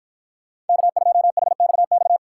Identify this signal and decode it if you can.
Morse code, heard as a keyed tone, about 52 wpm, 700 Hz. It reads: K2HXX